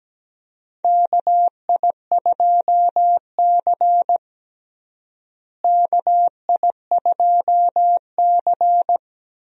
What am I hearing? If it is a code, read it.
Morse code, heard as a keyed tone, about 17 wpm, 700 Hz. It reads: KI2C KI2C
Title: USA Call Signs - 4 Characters - Repeated 2x - 17wpm